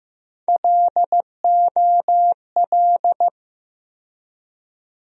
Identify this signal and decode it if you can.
Morse code, heard as a keyed tone, about 15 wpm, 700 Hz. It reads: LOL
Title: USA Call Sign Postfixes - 3 Letters 15wpm